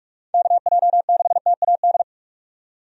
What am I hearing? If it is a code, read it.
Morse code, heard as a keyed tone, about 42 wpm, 700 Hz. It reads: KJ6TAD
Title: USA Call Signs - 6 Characters - 45wpm